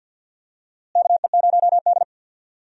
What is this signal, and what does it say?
Morse code, heard as a keyed tone, about 52 wpm, 700 Hz. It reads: KE0D
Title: USA Call Signs - 4 Characters - Repeated 3x - 50wpm